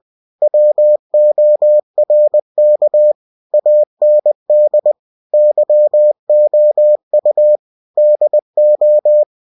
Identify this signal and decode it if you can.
Morse code, heard as a keyed tone, about 20 wpm, 600 Hz. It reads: WORK AND YOU DO
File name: cw-books / winnie-the-pooh / winnie-20wpm-page-0113